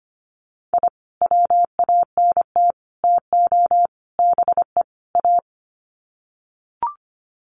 Morse code, heard as a keyed tone, about 25 wpm, 700 Hz. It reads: I WANT TO BE A
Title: Sets of 5 Words Encore 25wpm